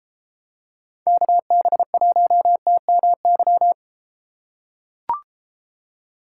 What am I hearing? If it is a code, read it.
Morse code, heard as a keyed tone, about 31 wpm, 700 Hz. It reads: KB1TMY